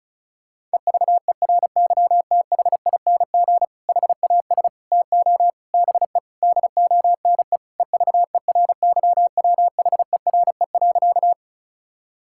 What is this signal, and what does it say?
Morse code, heard as a keyed tone, about 36 wpm, 700 Hz. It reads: EVERYTHING HAS TO BE DONE EVERYWHERE.
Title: Sentences from Top 1600 Words 35wpm